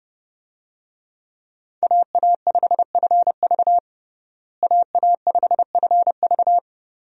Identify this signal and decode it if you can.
Morse code, heard as a keyed tone, about 30 wpm, 700 Hz. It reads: AA5FV AA5FV
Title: USA Call Signs - 5 Characters - Repeated 2x - 30wpm